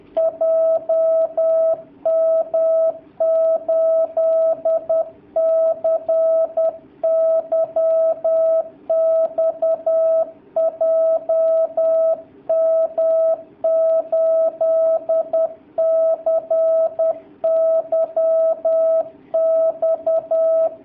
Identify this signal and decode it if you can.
Morse code, heard as a keyed tone, about 10 wpm, 650 Hz. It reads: JM8CYX JM8CYX
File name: jh8cbh / 2019cw / sp10- w